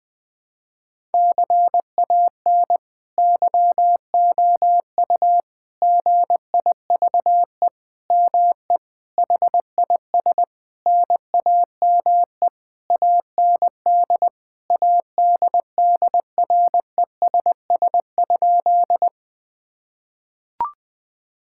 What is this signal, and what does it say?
Morse code, heard as a keyed tone, about 21 wpm, 700 Hz. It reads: CAN YOU GIVE ME HIS NAME AND ADDRESS?
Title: Sentences from Top 1000 Words - Review Part 6 - 20wpm